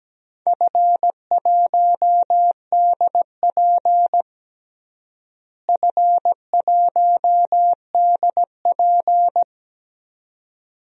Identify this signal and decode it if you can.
Morse code, heard as a keyed tone, about 17 wpm, 700 Hz. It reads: F1DP F1DP
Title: DXCC-100 Call Signs - 4 Characters - Repeated 2x - 17wpm